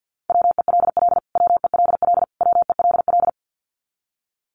Morse code, heard as a keyed tone, about 50 wpm, 700 Hz. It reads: WELL WELL WELL
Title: Top 200 Words Repeated 50wpm